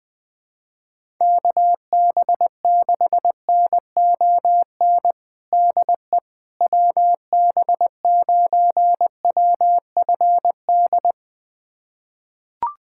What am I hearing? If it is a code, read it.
Morse code, heard as a keyed tone, about 20 wpm, 700 Hz. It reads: KB6NON DE WB9WFD
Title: USA Callsign DE Callsign 20wpm